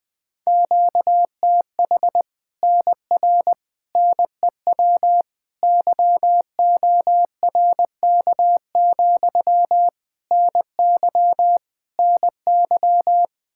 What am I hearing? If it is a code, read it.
Morse code, heard as a keyed tone, about 20 wpm, 700 Hz. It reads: QTH NR NEW YORK, NY NY
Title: Practice QSO - Exchange 1 - Beginning - 20wpm